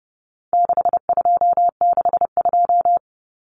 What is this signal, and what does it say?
Morse code, heard as a keyed tone, about 30 wpm, 700 Hz. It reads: 6262
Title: Numbers - 4-Digit 30wpm